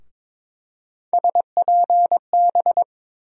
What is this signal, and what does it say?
Morse code, heard as a keyed tone, about 22 wpm, 700 Hz. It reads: SPB
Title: USA Call Sign Postfixes - 3 Letters 22wpm